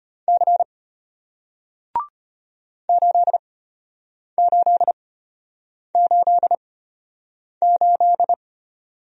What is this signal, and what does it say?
Morse code, heard as a keyed tone, about 33 wpm, spaced as for 8 wpm, 700 Hz. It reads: C 8888